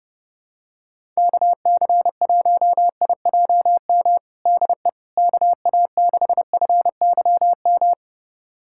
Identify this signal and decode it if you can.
Morse code, heard as a keyed tone, about 29 wpm, 700 Hz. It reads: KC1IJM DE KA6FYM
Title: USA Callsign DE Callsign - 6 Characters - 30wpm